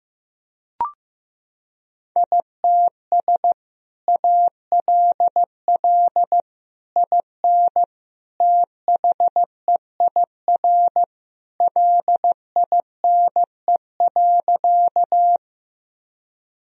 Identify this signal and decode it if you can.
Morse code, heard as a keyed tone, about 15 wpm, 700 Hz. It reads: ITS ALL IN THEIR LINE.
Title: Sentences from Top 1000 Words - Review Part 1 - 15wpm